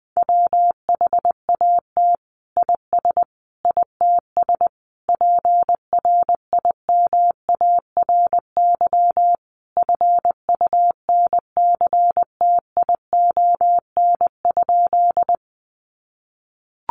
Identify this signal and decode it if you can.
Morse code, heard as a keyed tone, about 20 wpm, 700 Hz. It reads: WHAT IS ITS PRIMARY FUNCTION?
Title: Sentences from Top 1100 Words 20wpm